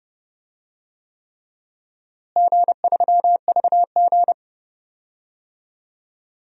Morse code, heard as a keyed tone, about 30 wpm, 700 Hz. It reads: G3VG
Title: DXCC-100 Call Signs - 4 Characters - Repeated 3x - 30wpm